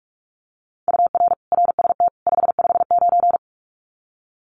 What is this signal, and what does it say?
Morse code, heard as a keyed tone, about 44 wpm, 700 Hz. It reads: UR RST 559
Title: Common QSO Phrases 45wpm